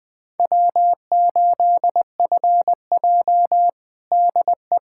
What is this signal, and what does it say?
Morse code, heard as a keyed tone, about 21 wpm, 700 Hz. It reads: W8FJ DE
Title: Practice QSO - Exchange 4 - Ending - 20wpm